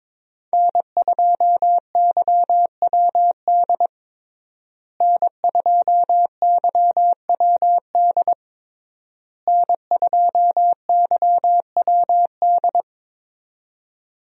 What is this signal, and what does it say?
Morse code, heard as a keyed tone, about 21 wpm, 700 Hz. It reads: N2YWD N2YWD N2YWD
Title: USA Call Signs - 5 Characters - Repeated 3x - 22wpm